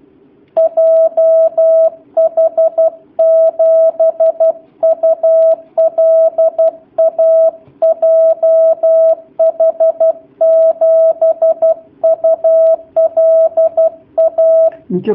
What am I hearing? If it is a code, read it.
Morse code, heard as a keyed tone, about 12 wpm, 650 Hz. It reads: JH7ULA JH7ULA
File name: jh8cbh / 2019cw / sp11- w